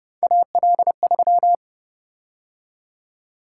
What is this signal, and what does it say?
Morse code, heard as a keyed tone, about 29 wpm, 700 Hz. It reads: AL3